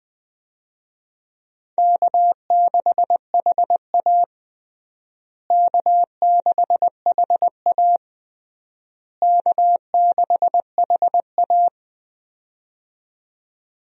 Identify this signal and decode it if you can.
Morse code, heard as a keyed tone, about 21 wpm, 700 Hz. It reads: K6HA K6HA K6HA
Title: USA Call Signs - 4 Characters - Repeated 3x - 20wpm